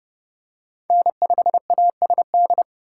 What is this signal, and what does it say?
Morse code, heard as a keyed tone, about 27 wpm, 700 Hz. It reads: N5ASD